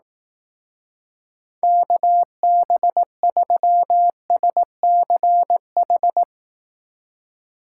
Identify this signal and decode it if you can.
Morse code, heard as a keyed tone, about 18 wpm, 700 Hz. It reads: KB3SCH
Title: Speed Racing - USA 2x3 Call Signs 15wpm